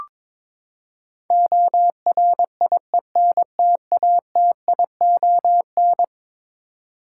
Morse code, heard as a keyed tone, about 22 wpm, 700 Hz. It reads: ORIENTATION
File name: practice / 22wpm / Eleven Letter Words - 22wpm